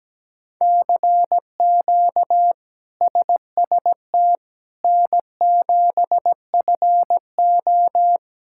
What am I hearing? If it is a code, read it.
Morse code, heard as a keyed tone, about 17 wpm, 700 Hz. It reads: CQ SST N7FO
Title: K1USN SST QSO Practice - 17wpm